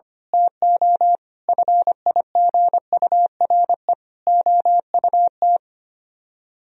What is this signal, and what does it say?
Morse code, heard as a keyed tone, about 24 wpm, 700 Hz. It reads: TO FIGURE OUT
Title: Speed-Racing - Sets of 4 Words 25wpm